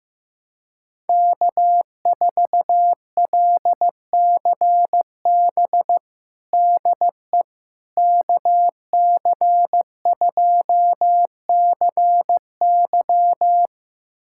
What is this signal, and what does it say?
Morse code, heard as a keyed tone, about 16 wpm, 700 Hz. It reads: K4LCB DE KC2CY
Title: USA Callsign DE Callsign - 5 Characters - 15wpm